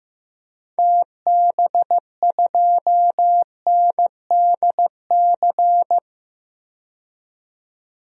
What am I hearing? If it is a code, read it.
Morse code, heard as a keyed tone, about 15 wpm, 700 Hz. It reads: TB2NDC
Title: DXCC-100 Call Signs - 6 Characters - 15wpm